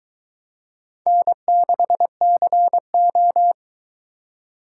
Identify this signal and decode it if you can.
Morse code, heard as a keyed tone, about 24 wpm, 700 Hz. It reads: N6CO